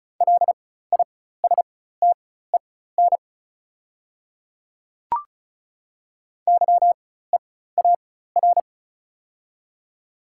Morse code, heard as a keyed tone, about 36 wpm, spaced as for 16 wpm, 690 Hz. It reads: LISTEN YEAR
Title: 4x Character Spacing - Top 500 Words - 35wpm